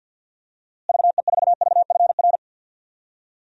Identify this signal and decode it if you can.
Morse code, heard as a keyed tone, about 50 wpm, 700 Hz. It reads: VE4VFR
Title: DXCC-100 Call Signs - 6 Characters - Repeated 2x - 50wpm